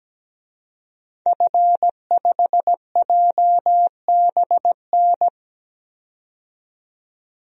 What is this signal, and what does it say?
Morse code, heard as a keyed tone, about 17 wpm, 700 Hz. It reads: F5JBN
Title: DXCC-100 Call Signs - 5 Characters - 17wpm